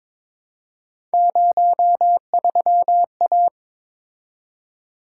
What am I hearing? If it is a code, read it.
Morse code, heard as a keyed tone, about 21 wpm, 700 Hz. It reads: 03A
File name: practice / 22wpm / Three Letter-Number Combinations - 22wpm